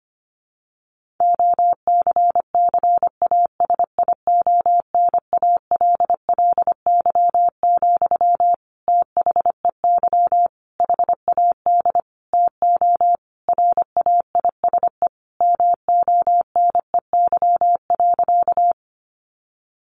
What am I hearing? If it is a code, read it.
Morse code, heard as a keyed tone, about 25 wpm, 700 Hz. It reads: OCCASIONALLY, THEY HAD TO RAISE MONEY.
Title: Sentences from Top 1400 Words 25wpm